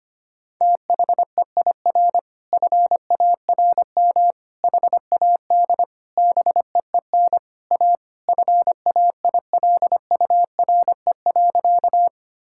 Morse code, heard as a keyed tone, about 25 wpm, 700 Hz. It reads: THEIR FARM HAD BEEN A FAILURE.
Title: Sentences from Top 1000 Words - Review Part 6 - 25wpm